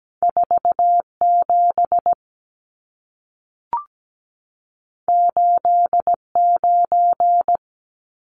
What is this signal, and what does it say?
Morse code, heard as a keyed tone, about 17 wpm, 700 Hz. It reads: 47 89
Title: Numbers - 2-Digit 17wpm